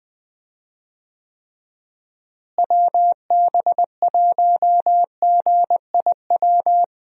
Morse code, heard as a keyed tone, about 20 wpm, 700 Hz. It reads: WB1GIW
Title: USA Call Signs - 6 Characters - Repeated 3x - 20wpm